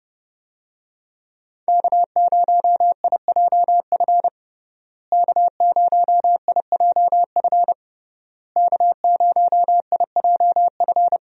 Morse code, heard as a keyed tone, about 30 wpm, 700 Hz. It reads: K0IJF K0IJF K0IJF
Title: USA Call Signs - 5 Characters - Repeated 3x - 30wpm